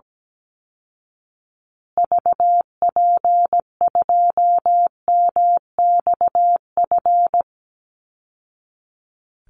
Morse code, heard as a keyed tone, about 17 wpm, 700 Hz. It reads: VP2MXF
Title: Call Signs Repeated 3x - 17wpm